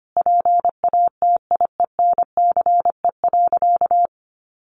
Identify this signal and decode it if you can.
Morse code, heard as a keyed tone, about 25 wpm, 700 Hz. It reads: PATIENCE.
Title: Sentences from Top 1300 Words 25wpm